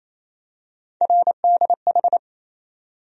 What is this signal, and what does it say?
Morse code, heard as a keyed tone, about 28 wpm, 700 Hz. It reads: RDH
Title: USA Call Sign Postfixes - 3 Letters 28wpm